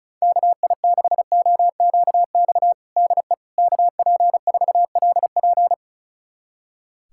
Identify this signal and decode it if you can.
Morse code, heard as a keyed tone, about 36 wpm, 700 Hz. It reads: KI6OQX DE KP4LP
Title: USA Callsign DE Callsign 35wpm